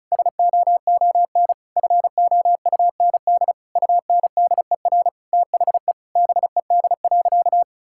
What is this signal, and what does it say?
Morse code, heard as a keyed tone, about 36 wpm, 700 Hz. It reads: SOON FOUND UNDER THE BED.
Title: Sentences from Top 1000 Words - Review Part 5 - 35wpm